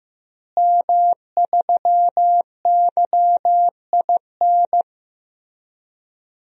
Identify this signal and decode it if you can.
Morse code, heard as a keyed tone, about 15 wpm, 700 Hz. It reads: M3YIN
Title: DXCC-100 Call Signs - 5 Characters - Repeated 2x - 15wpm